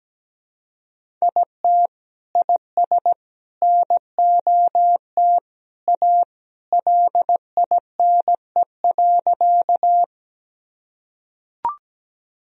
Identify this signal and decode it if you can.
Morse code, heard as a keyed tone, about 17 wpm, 700 Hz. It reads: IT IS NOT A LINE.
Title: Sentences from Top 1000 Words - Review Part 1 - 17wpm